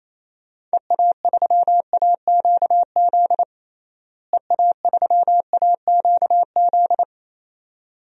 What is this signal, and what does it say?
Morse code, heard as a keyed tone, about 28 wpm, 700 Hz. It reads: EA3AQZ EA3AQZ
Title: DXCC-100 Call Signs - 6 Characters - Repeated 2x - 28wpm